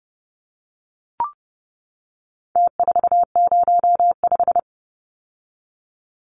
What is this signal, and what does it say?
Morse code, heard as a keyed tone, about 30 wpm, 700 Hz. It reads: T405
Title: Numbers - Contest Style - 4-Digit 30wpm